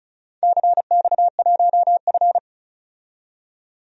Morse code, heard as a keyed tone, about 36 wpm, 690 Hz. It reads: CX1F
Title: DXCC-100 Call Signs - 4 Characters - 35wpm